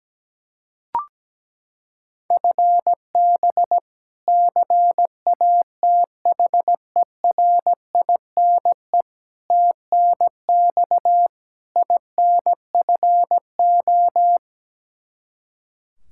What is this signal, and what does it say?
Morse code, heard as a keyed tone, about 17 wpm, 700 Hz. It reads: FB CATHERINE TNX INFO
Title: K1USN SST QSO Practice 17wpm